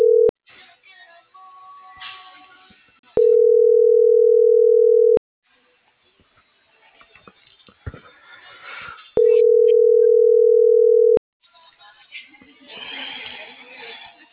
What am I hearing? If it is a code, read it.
Ringback tone (North America), 440 + 480 Hz.